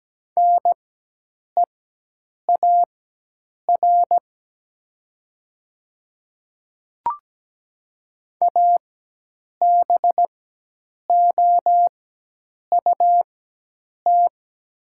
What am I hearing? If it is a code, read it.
Morse code, heard as a keyed tone, about 17 wpm, spaced as for 8 wpm, 700 Hz. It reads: NEAR ABOUT
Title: 4x Character Spacing - Top 200 Words - 17wpm